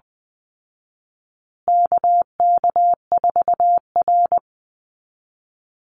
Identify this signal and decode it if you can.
Morse code, heard as a keyed tone, about 20 wpm, 700 Hz. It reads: KK4R
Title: Call Signs Repeated 2x - 20wpm